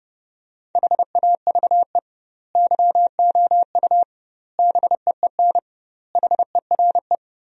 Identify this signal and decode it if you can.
Morse code, heard as a keyed tone, about 30 wpm, 700 Hz. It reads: HAVE YOU BEEN HERE
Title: 2x Word-Spacing - Sentences from Top 200 Words - 30wpm